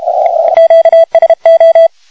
Morse code, heard as a keyed tone, about 32 wpm, 3300 Hz, with radio noise. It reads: QSO
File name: Pile up OK1DCS